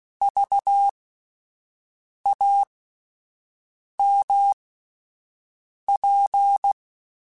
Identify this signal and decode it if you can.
Morse code, heard as a keyed tone, about 16 wpm, spaced as for 5 wpm, 800 Hz. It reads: VAMP